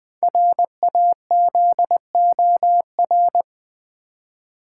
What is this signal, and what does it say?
Morse code, heard as a keyed tone, about 20 wpm, 700 Hz. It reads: RAZOR
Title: Easy Words with Hints - 20wpm